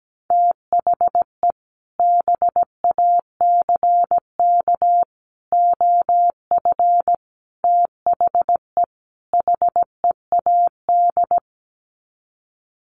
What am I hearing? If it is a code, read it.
Morse code, heard as a keyed tone, about 17 wpm, 700 Hz. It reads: THE BACK OF THE HEAD
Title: Sets of 5 Words Encore2 17wpm